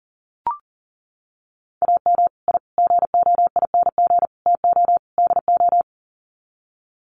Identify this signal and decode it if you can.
Morse code, heard as a keyed tone, about 40 wpm, 700 Hz. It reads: AM I GOING TO DO
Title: Sets of 5 Words Encore 40wpm